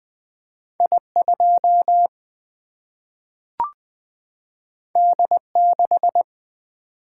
Morse code, heard as a keyed tone, about 20 wpm, 700 Hz. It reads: I2 D6
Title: Single-Letter Single-Number 20wpm